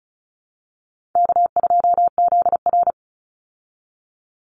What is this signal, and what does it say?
Morse code, heard as a keyed tone, about 34 wpm, 700 Hz. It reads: K2ZR
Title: Call Signs Repeated 2x - 35wpm